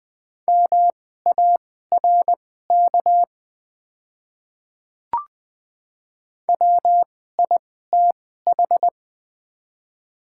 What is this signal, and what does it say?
Morse code, heard as a keyed tone, about 20 wpm, spaced as for 15 wpm, 700 Hz. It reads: MARK WITH
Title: 2x Character Spacing - Top 300 Words - 20wpm